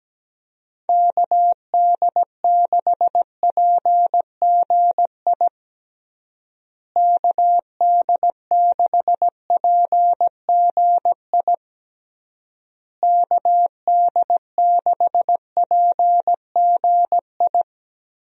Morse code, heard as a keyed tone, about 17 wpm, 700 Hz. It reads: KD6PGI KD6PGI KD6PGI
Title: USA Call Signs - 6 Characters - Repeated 3x - 17wpm